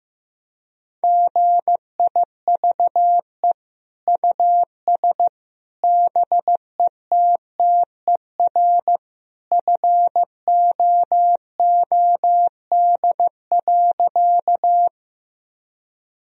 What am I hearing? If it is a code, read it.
Morse code, heard as a keyed tone, about 15 wpm, 700 Hz. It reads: GIVE US BETTER FOOD.
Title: Sentences from Top 1000 Words - Review Part 1 - 15wpm